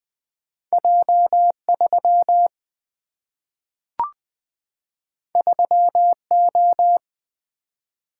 Morse code, heard as a keyed tone, about 20 wpm, 700 Hz. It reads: J 3 3 O
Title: Single-Letter Single-Number 20wpm